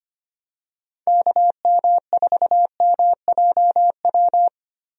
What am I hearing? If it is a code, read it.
Morse code, heard as a keyed tone, about 25 wpm, 700 Hz. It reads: KM4MJW
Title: USA Call Signs - 6 Characters - Repeated 3x - 25wpm